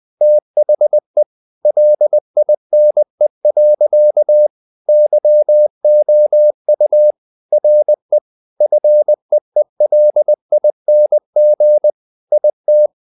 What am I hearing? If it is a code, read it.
Morse code, heard as a keyed tone, about 20 wpm, 600 Hz. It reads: THE LINE. YOU RE FEELING IT
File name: the old man and the sea / the old man and the sea part 1 20wpm